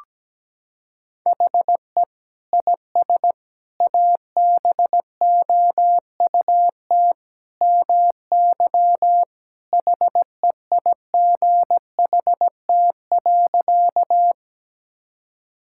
Morse code, heard as a keyed tone, about 17 wpm, 700 Hz. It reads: HE IS ABOUT MY HEIGHT.
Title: Sentences from Top 1600 Words 17wpm